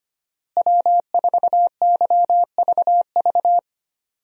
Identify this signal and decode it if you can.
Morse code, heard as a keyed tone, about 25 wpm, 700 Hz. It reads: W4YVV
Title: USA Call Signs - 5 Characters - Repeated 2x - 25wpm